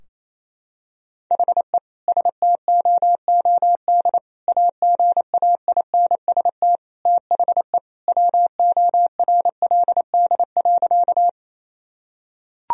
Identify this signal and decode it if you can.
Morse code, heard as a keyed tone, about 28 wpm, 700 Hz. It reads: HE STOOD AGAINST THE WORLD.
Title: Sentences from Top 1000 Words - Review Part 2 - 28wpm